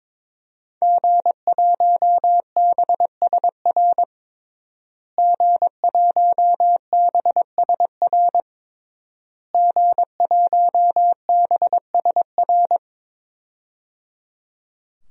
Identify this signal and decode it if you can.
Morse code, heard as a keyed tone, about 22 wpm, 700 Hz. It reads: G1BSR G1BSR G1BSR
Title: DXCC-100 Call Signs - 5 Characters - Repeated 3x - 22wpm